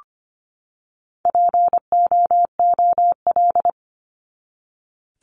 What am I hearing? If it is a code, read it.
Morse code, heard as a keyed tone, about 25 wpm, 700 Hz. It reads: POOL
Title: Top 1751-2000 Words 25wpm